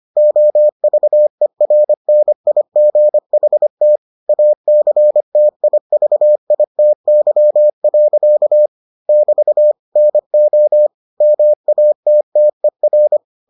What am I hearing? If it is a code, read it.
Morse code, heard as a keyed tone, about 25 wpm, 600 Hz. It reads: OVERNIGHT ACTIVITY. <BT> NO MATTER